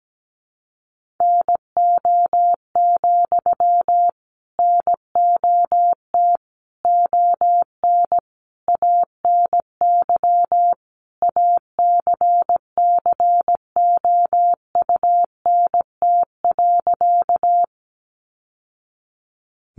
Morse code, heard as a keyed tone, about 17 wpm, 700 Hz. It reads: NO, NOT ON ANY ACCOUNT.
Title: Sentences from Top 900 Words 17wpm